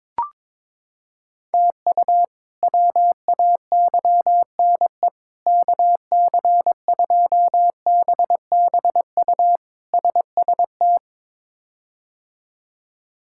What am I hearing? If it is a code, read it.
Morse code, heard as a keyed tone, about 22 wpm, 700 Hz. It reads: TU WAYNE KC2BBU SST
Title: K1USN SST QSO Practice - 22wpm